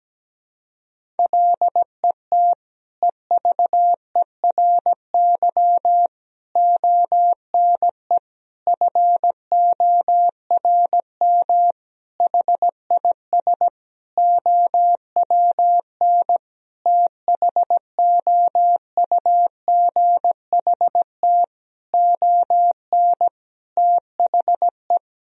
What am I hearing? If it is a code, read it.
Morse code, heard as a keyed tone, about 17 wpm, 700 Hz. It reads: LET EVERY ONE FORM HIS OWN THOUGHT ON THE
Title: Sentences from Top 1000 Words - Review Part 1 - 17wpm